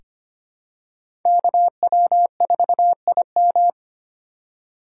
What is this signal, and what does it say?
Morse code, heard as a keyed tone, about 25 wpm, 700 Hz. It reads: KW4IM
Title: USA Call Signs - 3x Character Spacing - 25wpm